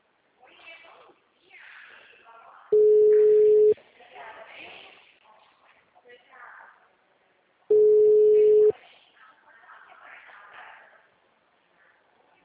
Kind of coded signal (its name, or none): telephone call progress tone